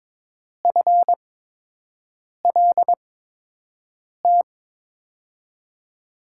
Morse code, heard as a keyed tone, about 22 wpm, spaced as for 6 wpm, 700 Hz. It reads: FLT
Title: USA Call Signs - 5 Characters - 8x Character Spacing - 22wpm